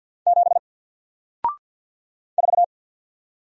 Morse code, heard as a keyed tone, about 48 wpm, 700 Hz. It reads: 7 4
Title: Single Letter-Number Alt 50wpm